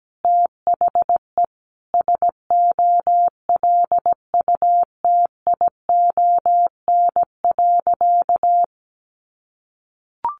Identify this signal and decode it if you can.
Morse code, heard as a keyed tone, about 17 wpm, 700 Hz. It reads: THE SOLUTION.
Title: Sentences from Top 1300 Words 17wpm